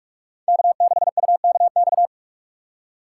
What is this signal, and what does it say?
Morse code, heard as a keyed tone, about 46 wpm, 690 Hz. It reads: K6UKX
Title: USA Call Signs - 5 Characters - 2x Character Spacing - 45wpm